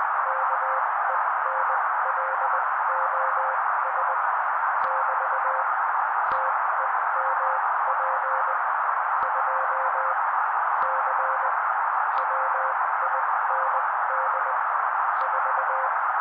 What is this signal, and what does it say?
Morse code, heard as a keyed tone, about 20 wpm, spaced as for 16 wpm, 530 Hz. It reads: KENLOS <BT> TEMP 2 C WIND 4
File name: daily / 16wpm noise